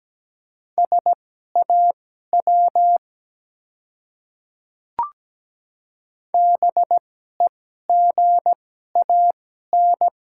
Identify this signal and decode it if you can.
Morse code, heard as a keyed tone, about 17 wpm, spaced as for 12 wpm, 700 Hz. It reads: SAW BEGAN
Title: 2x Character Spacing - Top 300 Words - 17wpm